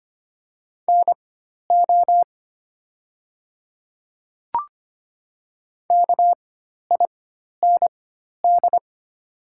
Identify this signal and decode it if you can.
Morse code, heard as a keyed tone, about 25 wpm, spaced as for 12 wpm, 690 Hz. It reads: NO KIND
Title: 4x Character Spacing - Top 200 Words - 25wpm